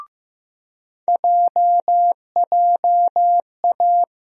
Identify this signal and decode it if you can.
Morse code, heard as a keyed tone, about 15 wpm, 700 Hz. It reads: JJA
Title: USA Call Sign Postfixes - 3 Letters 15wpm